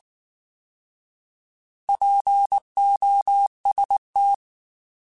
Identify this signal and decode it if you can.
Morse code, heard as a keyed tone, about 19 wpm, 800 Hz. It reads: POST
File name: Alphabetical / Alphabetical mp3s / Ps @18